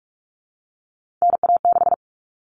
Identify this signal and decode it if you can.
Morse code, heard as a keyed tone, about 44 wpm, 700 Hz. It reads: NA6
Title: Common Cut Numbers - 3-Digit 45wpm